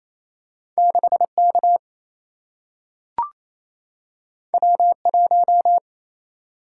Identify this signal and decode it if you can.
Morse code, heard as a keyed tone, about 28 wpm, 700 Hz. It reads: 6K W1
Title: Single-Letter Single-Number 28wpm